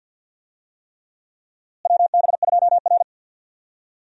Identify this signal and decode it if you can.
Morse code, heard as a keyed tone, about 51 wpm, 700 Hz. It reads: WB2R
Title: USA Call Signs - 4 Characters - Repeated 3x - 50wpm